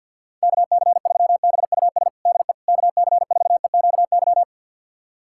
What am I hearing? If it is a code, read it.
Morse code, heard as a keyed tone, about 50 wpm, 700 Hz. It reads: KC3BUS DE KC4EQY
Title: USA Callsign DE Callsign 50wpm